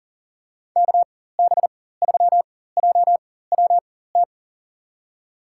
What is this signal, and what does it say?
Morse code, heard as a keyed tone, about 40 wpm, spaced as for 19 wpm, 700 Hz. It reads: KB3JWT
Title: USA Call Signs - 4x Character Spacing - 40wpm